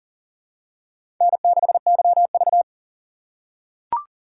Morse code, heard as a keyed tone, about 42 wpm, 700 Hz. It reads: N6YV
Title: USA Call Signs - 4 Characters - 3x Character Spacing - 40wpm